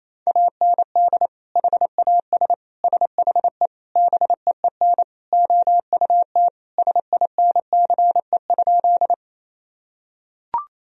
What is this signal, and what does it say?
Morse code, heard as a keyed tone, about 28 wpm, 700 Hz. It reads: AND HAS SHE BEEN OUT SINCE?
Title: Sentences from Top 1000 Words - Review Part 1 - 28wpm